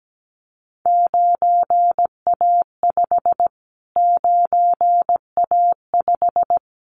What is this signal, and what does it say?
Morse code, heard as a keyed tone, about 17 wpm, 700 Hz. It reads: 9A5 9A5
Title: Callsign Prefixes for Top 25 DXCC - Repeated 3x - 17wpm